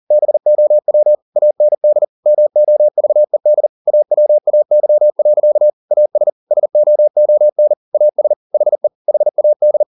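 Morse code, heard as a keyed tone, about 40 wpm, 600 Hz. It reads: BOW AND MOVED AWAY. AS SOON AS HE HAD
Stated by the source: Morse code at 40 wpm, practice text, CW med Gnister.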